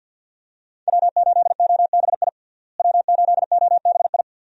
Morse code, heard as a keyed tone, about 47 wpm, 700 Hz. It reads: W8OBI W8OBI
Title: USA Call Signs - 5 Characters - Repeated 2x - 50wpm